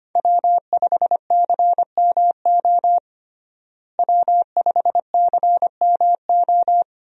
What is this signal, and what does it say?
Morse code, heard as a keyed tone, about 25 wpm, 700 Hz. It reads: W5CMO W5CMO
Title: USA Call Signs - 5 Characters - Repeated 2x - 25wpm